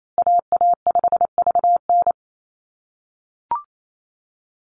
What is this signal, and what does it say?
Morse code, heard as a keyed tone, about 28 wpm, 700 Hz. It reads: AA5VN
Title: USA Call Signs Repeated 3x 28wpm